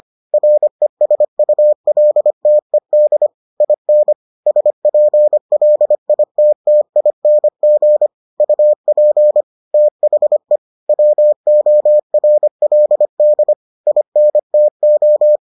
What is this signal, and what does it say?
Morse code, heard as a keyed tone, about 25 wpm, 600 Hz. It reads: RESULTED IN SPLITTING UP THE WORLD INTO